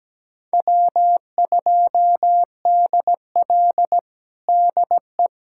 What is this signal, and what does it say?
Morse code, heard as a keyed tone, about 17 wpm, 700 Hz. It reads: W2DL DE
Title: USA Callsign DE Callsign - 4 Characters - 17wpm